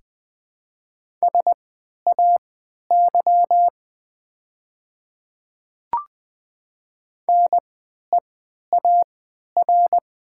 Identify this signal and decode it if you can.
Morse code, heard as a keyed tone, about 20 wpm, spaced as for 11 wpm, 700 Hz. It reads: SAY NEAR
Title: 3x Character Spacing - Top 300 Words - 20wpm